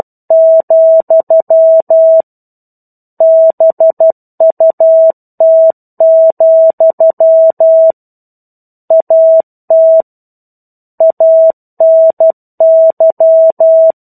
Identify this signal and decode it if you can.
Morse code, heard as a keyed tone, about 12 wpm, 650 Hz. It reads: , BUT, AT ANY
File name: Chapter-01 12 WPM